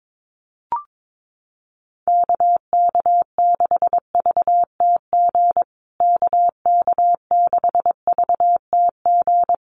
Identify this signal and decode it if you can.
Morse code, heard as a keyed tone, about 22 wpm, 700 Hz. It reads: KK6VTG KK6VTG
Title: USA Call Signs Repeated 2x 22wpm